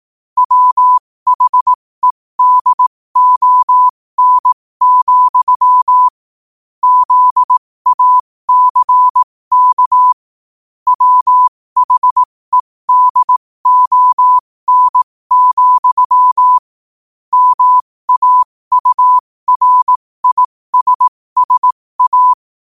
Morse code, heard as a keyed tone, about 18 wpm, spaced as for 15 wpm, 1000 Hz. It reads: WHEDON, ZACK WHEDON, MAURISSA
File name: qotdfiles / 15wpm